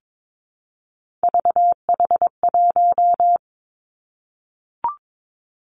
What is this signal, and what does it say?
Morse code, heard as a keyed tone, about 22 wpm, 700 Hz. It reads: VH1
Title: Three Challenging Letter-Number Combinations 22wpm